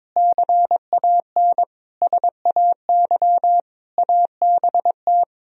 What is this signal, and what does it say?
Morse code, heard as a keyed tone, about 22 wpm, 700 Hz. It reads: CAN SAY ABT
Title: Sentences from Top 200 Words and Abbreviations 22wpm